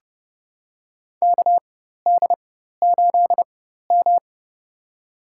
Morse code, heard as a keyed tone, about 30 wpm, spaced as for 14 wpm, 700 Hz. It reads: KD8M
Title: USA Call Signs - 4x Character Spacing - 30wpm